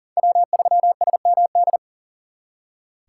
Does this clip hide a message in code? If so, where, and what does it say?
Yes: Morse code from 0.2 to 1.8 s, heard as a keyed tone, about 51 wpm, 700 Hz. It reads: W3SMD